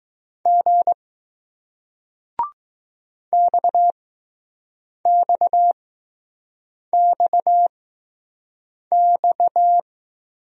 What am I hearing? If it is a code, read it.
Morse code, heard as a keyed tone, about 19 wpm, spaced as for 6 wpm, 700 Hz. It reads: G XXXX